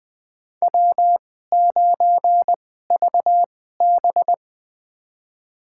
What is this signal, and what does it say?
Morse code, heard as a keyed tone, about 20 wpm, spaced as for 14 wpm, 700 Hz. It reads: W9VB
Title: USA Call Signs - 4 Characters - 2x Character Spacing - 20wpm